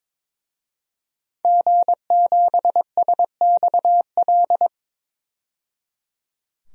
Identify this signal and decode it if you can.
Morse code, heard as a keyed tone, about 22 wpm, 700 Hz. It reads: G7SXL
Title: DXCC-100 Call Signs - 5 Characters - 22wpm